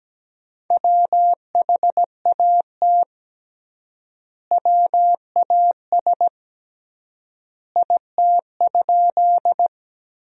Morse code, heard as a keyed tone, about 17 wpm, 700 Hz. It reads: WHAT WAS IT?
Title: 3x Word-Spacing - Sentences from Top 100 Words - 17wpm